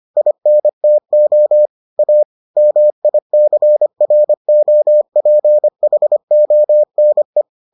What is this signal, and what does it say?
Morse code, heard as a keyed tone, about 25 wpm, 600 Hz. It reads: INTO A MICROPHONE